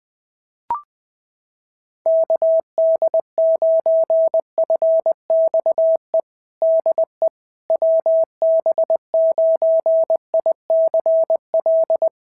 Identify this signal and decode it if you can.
Morse code, heard as a keyed tone, about 20 wpm, 650 Hz. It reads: KD9FXE DE WB9ICL
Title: SOTA QSO Practice 20wpm